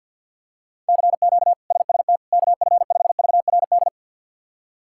Morse code, heard as a keyed tone, about 51 wpm, 700 Hz. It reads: CQ SST KF5VRD
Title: K1USN SST QSO Practice - 50wpm